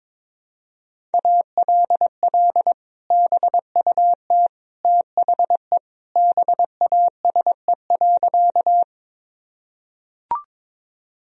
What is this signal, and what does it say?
Morse code, heard as a keyed tone, about 22 wpm, 700 Hz. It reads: ALL BUT THE BASE.
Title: Sentences from Top 1000 Words - Review Part 1 - 22wpm